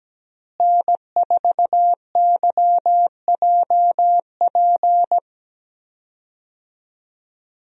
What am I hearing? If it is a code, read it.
Morse code, heard as a keyed tone, about 17 wpm, 700 Hz. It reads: N4YJP
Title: USA Call Signs - 5 Characters - Repeated 3x - 17wpm